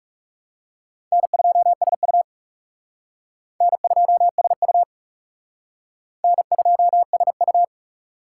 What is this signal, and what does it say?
Morse code, heard as a keyed tone, about 36 wpm, 700 Hz. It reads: N2SU N2SU N2SU
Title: Speed Racing - USA Call Signs 30wpm